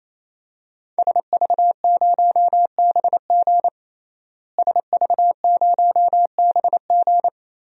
Morse code, heard as a keyed tone, about 28 wpm, 700 Hz. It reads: SV0BG SV0BG
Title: DXCC-100 Call Signs - 5 Characters - Repeated 3x - 28wpm